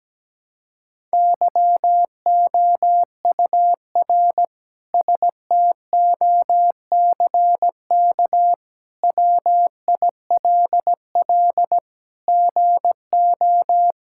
Morse code, heard as a keyed tone, about 17 wpm, 700 Hz. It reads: YOUR STOCK WILL GO
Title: Sentences from Top 1000 Words - Review Part 5 - 17wpm